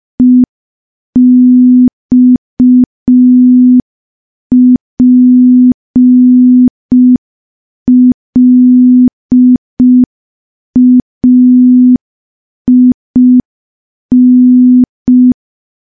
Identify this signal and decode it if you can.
Morse code, heard as a keyed tone, about 5 wpm, 250 Hz. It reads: EXPLAIN
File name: morse / morse-training-05WPM